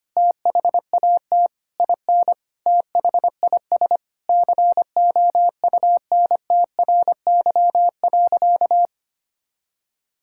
Morse code, heard as a keyed tone, about 25 wpm, 700 Hz. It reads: THAT IN THIS COUNTRY.
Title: Sentences from Top 1000 Words - Review Part 5 - 25wpm